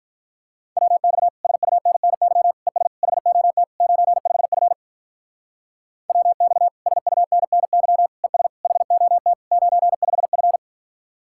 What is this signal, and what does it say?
Morse code, heard as a keyed tone, about 42 wpm, 700 Hz. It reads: WX SUNNY ES HOT 95F WX SUNNY ES HOT 95F
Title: Speed-Racing Common QSO Phrases - 35wpm